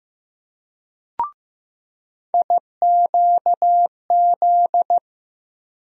Morse code, heard as a keyed tone, about 15 wpm, 700 Hz. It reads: IQZ
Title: USA Call Sign Postfixes - 3 Letters 15wpm